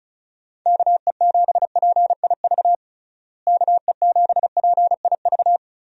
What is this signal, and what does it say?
Morse code, heard as a keyed tone, about 36 wpm, 700 Hz. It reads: KE7PIV KE7PIV
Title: USA Call Signs - 6 Characters - Repeated 3x - 35wpm